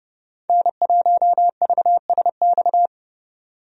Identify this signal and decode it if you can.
Morse code, heard as a keyed tone, about 30 wpm, 700 Hz. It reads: N1VSX
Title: USA Call Signs - 5 Characters - 30wpm